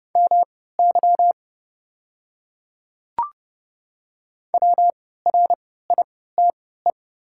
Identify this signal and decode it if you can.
Morse code, heard as a keyed tone, about 30 wpm, spaced as for 17 wpm, 700 Hz. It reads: MY WRITE
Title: 3x Character Spacing - Top 100 Words - 30wpm